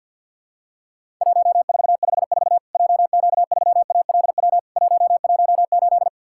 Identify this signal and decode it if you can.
Morse code, heard as a keyed tone, about 51 wpm, 700 Hz. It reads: 1454 JQ3ALW 118
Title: Worked All Europe DX Contest - QTC Exchange - 50wpm